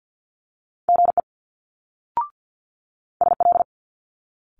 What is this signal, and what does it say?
Morse code, heard as a keyed tone, about 50 wpm, 700 Hz. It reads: GE SL
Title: Multi-letter Phonemes Quick Fire 50wpm